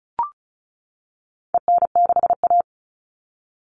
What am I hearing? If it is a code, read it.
Morse code, heard as a keyed tone, about 34 wpm, 700 Hz. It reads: EN6A
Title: All Cut Numbers - 4-Digit 35wpm